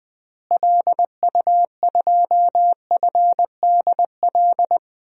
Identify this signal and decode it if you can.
Morse code, heard as a keyed tone, about 20 wpm, 700 Hz. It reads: LU2FDL
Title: DXCC-100 Call Signs - 6 Characters - Repeated 3x - 20wpm